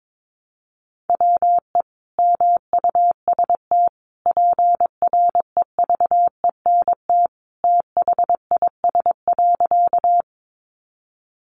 Morse code, heard as a keyed tone, about 22 wpm, 700 Hz. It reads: WE MUST PREVENT THIS.
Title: Sentences from Top 1100 Words 22wpm